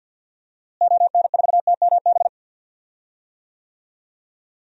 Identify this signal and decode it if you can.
Morse code, heard as a keyed tone, about 52 wpm, 700 Hz. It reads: ON4TMB
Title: DXCC-100 Call Signs - 6 Characters - Repeated 2x - 50wpm